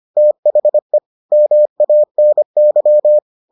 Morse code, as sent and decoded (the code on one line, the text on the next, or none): - .... / -- .- -. -.--
THE MANY